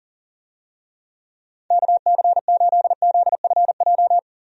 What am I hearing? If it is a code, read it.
Morse code, heard as a keyed tone, about 40 wpm, 700 Hz. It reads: KC8ZFJ
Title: USA Call Signs - 6 Characters - 40wpm